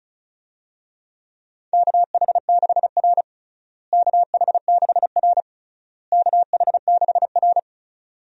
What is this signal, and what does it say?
Morse code, heard as a keyed tone, about 36 wpm, 700 Hz. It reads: KH6R KH6R KH6R
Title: USA Call Signs - 4 Characters - Repeated 3x - 35wpm